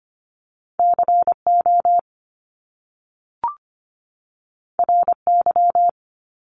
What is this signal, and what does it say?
Morse code, heard as a keyed tone, about 25 wpm, 700 Hz. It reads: CO RY
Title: Top Two Letter Combinations in Words 25wpm